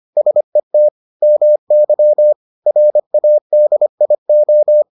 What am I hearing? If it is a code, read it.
Morse code, heard as a keyed tone, about 25 wpm, 600 Hz. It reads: SET MY RADIO